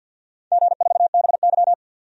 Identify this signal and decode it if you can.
Morse code, heard as a keyed tone, about 47 wpm, 700 Hz. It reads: G4BY
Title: DXCC-100 Call Signs - 4 Characters - 50wpm